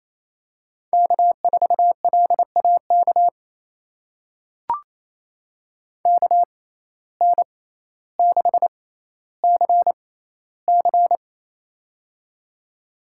Morse code, heard as a keyed tone, about 28 wpm, spaced as for 14 wpm, 700 Hz. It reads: K4LAK KN6CC
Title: USA Call Signs - 5 Characters - 6x Character Spacing - 28wpm